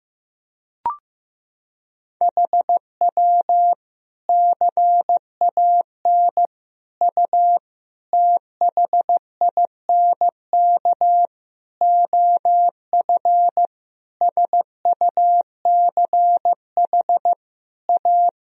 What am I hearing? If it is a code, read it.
Morse code, heard as a keyed tone, about 16 wpm, 700 Hz. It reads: HW CAN U THINK OF SUCH A
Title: Sentences from Top 200 Words and Abbreviations 15wpm